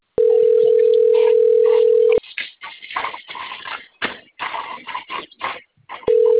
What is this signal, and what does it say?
Ringback tone (North America), 440 + 480 Hz.